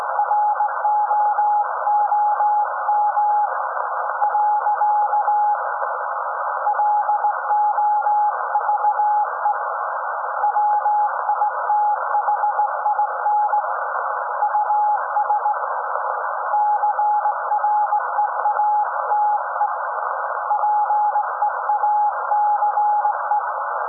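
Morse code, heard as a keyed tone, about 18 wpm, 840 Hz, with noise. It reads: ROOM . LOVE WITHIN US CREATE PEACE
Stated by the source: SAQ Grimeton Radio, Alexanderson 200 kW alternator on 17.2 kHz, received in Potsdam with a very strong signal on World Radio Day.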